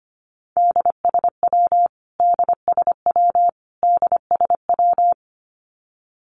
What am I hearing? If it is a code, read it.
Morse code, heard as a keyed tone, about 25 wpm, 700 Hz. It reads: DSW DSW DSW
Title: Abbreviations Repeated 3x - 25wpm